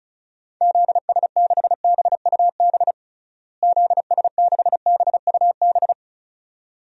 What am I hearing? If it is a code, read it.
Morse code, heard as a keyed tone, about 36 wpm, 700 Hz. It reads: ZS6BUB ZS6BUB
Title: DXCC-100 Call Signs - 6 Characters - Repeated 2x - 35wpm